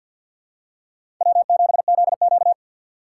Morse code, heard as a keyed tone, about 51 wpm, 700 Hz. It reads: W7ZQ
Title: USA Call Signs - 2x Character Spacing - 50wpm